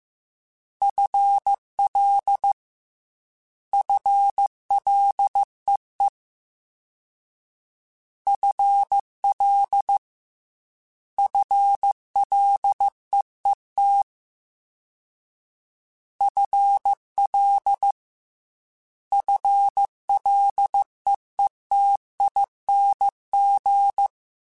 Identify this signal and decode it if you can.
Morse code, heard as a keyed tone, about 15 wpm, 800 Hz. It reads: FL FLEE FL FLEET FL FLEETING
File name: FL / FL mp3s / FL Prefixes 1x1 @13